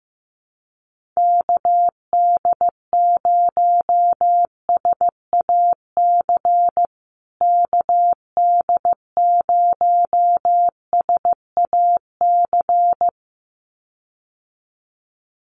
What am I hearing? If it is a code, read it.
Morse code, heard as a keyed tone, about 15 wpm, 700 Hz. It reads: KD0SAC KD0SAC
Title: USA Call Signs Repeated 2x 15wpm